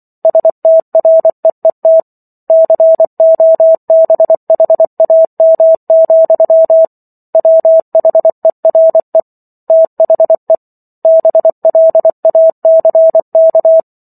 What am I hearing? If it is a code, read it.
Morse code, heard as a keyed tone, about 24 wpm, 650 Hz. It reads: STREET COBHAM, WHERE THE BLACK